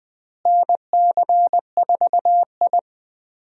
Morse code, heard as a keyed tone, about 20 wpm, 700 Hz. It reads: NC4I
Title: USA Call Signs - 4 Characters - Repeated 2x - 20wpm